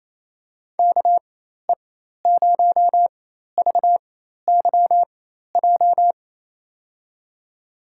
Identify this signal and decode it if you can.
Morse code, heard as a keyed tone, about 28 wpm, spaced as for 13 wpm, 700 Hz. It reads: KE0VYJ
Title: USA Call Signs - 4x Character Spacing - 28wpm